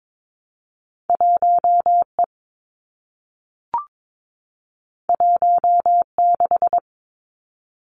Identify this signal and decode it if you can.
Morse code, heard as a keyed tone, about 22 wpm, 700 Hz. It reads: 1 E 1 6
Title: Two Challenging Letter-Number Combinations 22wpm